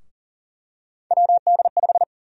Morse code, heard as a keyed tone, about 40 wpm, 700 Hz. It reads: WD5
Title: USA Call Sign Prefixes - Repeated 2x - 40wpm